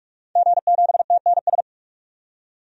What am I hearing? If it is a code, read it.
Morse code, heard as a keyed tone, about 46 wpm, 700 Hz. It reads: G7TNS
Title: DXCC-100 Call Signs - 5 Characters - 45wpm